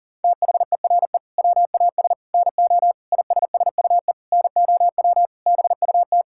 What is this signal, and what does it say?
Morse code, heard as a keyed tone, about 37 wpm, 700 Hz. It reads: THERE WAS NO ISSUE NOW BUT
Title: Sentences from Top 1000 Words - Review Part 6 - 40wpm